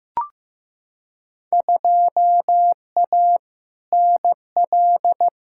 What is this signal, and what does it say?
Morse code, heard as a keyed tone, about 15 wpm, 700 Hz. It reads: 2A NL
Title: Field Day Exchanges Encore - 15wpm